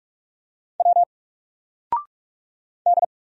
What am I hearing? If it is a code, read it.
Morse code, heard as a keyed tone, about 47 wpm, 700 Hz. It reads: W D
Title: Single Letter-Number Alt 45wpm